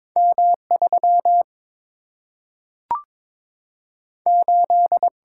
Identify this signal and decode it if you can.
Morse code, heard as a keyed tone, about 22 wpm, 700 Hz. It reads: M 3 8
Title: Single-Letter Single-Number 22wpm